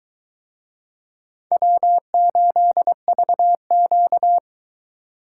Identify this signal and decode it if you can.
Morse code, heard as a keyed tone, about 22 wpm, 700 Hz. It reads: W8VQ